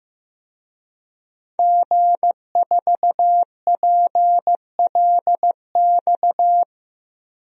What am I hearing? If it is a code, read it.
Morse code, heard as a keyed tone, about 15 wpm, 700 Hz. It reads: G4PLX
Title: DXCC-100 Call Signs - 5 Characters - Repeated 3x - 15wpm